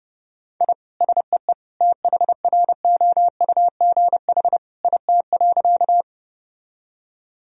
Morse code, heard as a keyed tone, about 30 wpm, 700 Hz. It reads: I SEE THROUGH IT.
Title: Sentences from Top 1000 Words - Review Part 1 - 30wpm